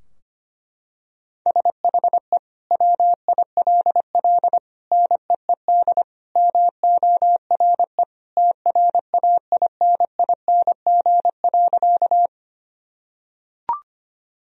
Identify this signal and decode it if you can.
Morse code, heard as a keyed tone, about 25 wpm, 700 Hz. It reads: SHE WILL NEED MORE TRAINING.
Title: Sentences from Top 1000 Words - Review Part 4 - 25wpm